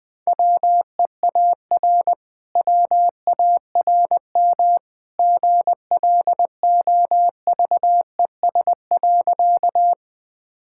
Morse code, heard as a keyed tone, about 20 wpm, 700 Hz. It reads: WEAR WARM GLOVES.
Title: Easy Sentences with Hints - 20wpm